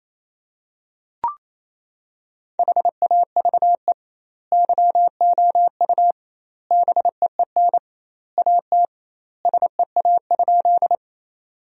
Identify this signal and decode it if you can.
Morse code, heard as a keyed tone, about 28 wpm, 700 Hz. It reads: HAVE YOU BEEN AT SEA?
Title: 2x Word-Spacing - Sentences from Top 300 Words - 28wpm